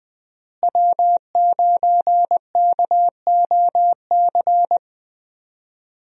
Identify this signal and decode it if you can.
Morse code, heard as a keyed tone, about 20 wpm, 700 Hz. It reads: W9KOC
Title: USA Call Signs - 5 Characters - Repeated 3x - 20wpm